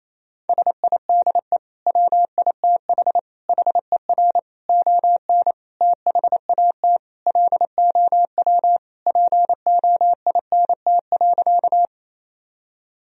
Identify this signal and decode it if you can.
Morse code, heard as a keyed tone, about 28 wpm, 700 Hz. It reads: SIDE WITH HER ON THAT LOW POINT.
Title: Sentences from Top 1000 Words - Review Part 1 - 28wpm